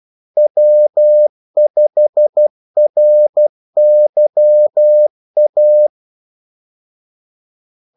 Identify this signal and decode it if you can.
Morse code, heard as a keyed tone, about 12 wpm, 600 Hz. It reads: W5RYA
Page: 2 ekstra ordmellemrum mellem kaldesignalerne